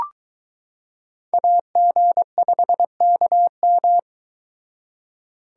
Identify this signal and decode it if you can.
Morse code, heard as a keyed tone, about 22 wpm, 700 Hz. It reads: AG5KM